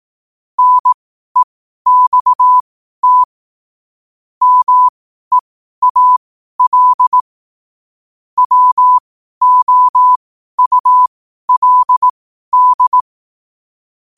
Morse code, heard as a keyed tone, about 18 wpm, spaced as for 12 wpm, 1000 Hz. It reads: NEXT MEAL WOULD